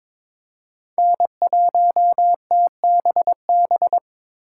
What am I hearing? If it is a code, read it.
Morse code, heard as a keyed tone, about 22 wpm, 700 Hz. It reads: N1TBB
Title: USA Call Signs - 5 Characters - 4x Character Spacing - 22wpm